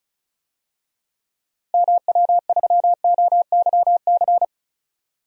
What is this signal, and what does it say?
Morse code, heard as a keyed tone, about 34 wpm, 700 Hz. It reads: MW3OYC
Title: DXCC-100 Call Signs - 6 Characters - Repeated 2x - 35wpm